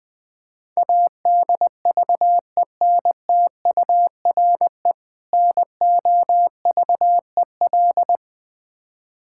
Morse code, heard as a keyed tone, about 20 wpm, 700 Hz. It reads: ADVENTURE NOVEL
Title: Difficult Words with Hints - 20wpm